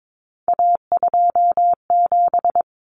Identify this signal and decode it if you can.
Morse code, heard as a keyed tone, about 22 wpm, 700 Hz. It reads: A27
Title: Common Cut Numbers - 3-Digit 22wpm